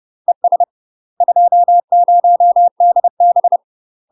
Morse code, heard as a keyed tone, about 30 wpm, 700 Hz. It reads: ES 20DB
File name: UK MASTER QSOs TRAINING / 30WPM / UK MASTER QSOs 0019